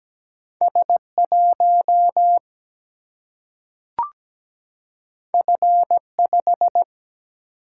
Morse code, heard as a keyed tone, about 17 wpm, 700 Hz. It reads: S 1 F 5
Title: Single-Letter Single-Number 17wpm